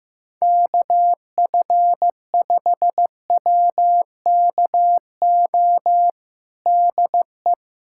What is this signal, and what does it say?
Morse code, heard as a keyed tone, about 15 wpm, 700 Hz. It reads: KF5WKO DE
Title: USA Callsign DE Callsign 15wpm